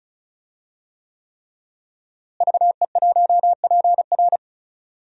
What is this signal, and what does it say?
Morse code, heard as a keyed tone, about 36 wpm, 700 Hz. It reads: VE1PR
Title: DXCC-100 Call Signs - 5 Characters - Repeated 3x - 35wpm